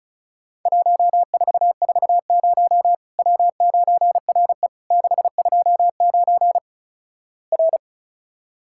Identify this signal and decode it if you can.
Morse code, heard as a keyed tone, about 35 wpm, 700 Hz. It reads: 1440 W9RE 629 R
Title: Worked All Europe DX Contest - QTC Exchange - 35wpm